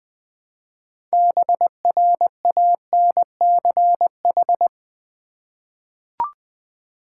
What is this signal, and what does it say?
Morse code, heard as a keyed tone, about 20 wpm, 700 Hz. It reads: BRANCH
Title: Six Letter Words - 20wpm